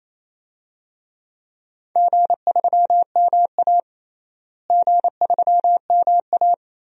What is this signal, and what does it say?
Morse code, heard as a keyed tone, about 28 wpm, 700 Hz. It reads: G3MA G3MA
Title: DXCC-100 Call Signs - 4 Characters - Repeated 2x - 28wpm